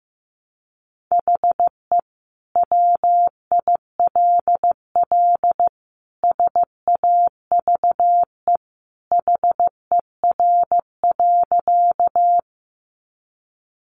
Morse code, heard as a keyed tone, about 15 wpm, 700 Hz. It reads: HE WILL SAVE HER.
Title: Sentences from Top 1100 Words 15wpm